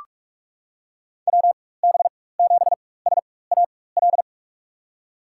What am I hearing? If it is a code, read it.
Morse code, heard as a keyed tone, about 46 wpm, spaced as for 21 wpm, 700 Hz. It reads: WB7SAL
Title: USA Call Signs - 6 Characters - 4x Character Spacing - 45wpm